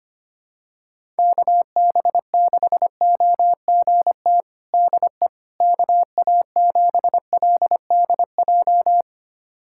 Morse code, heard as a keyed tone, about 25 wpm, 700 Hz. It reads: KB6OGT DE KA7LDJ
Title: USA Callsign DE Callsign - 6 Characters - 25wpm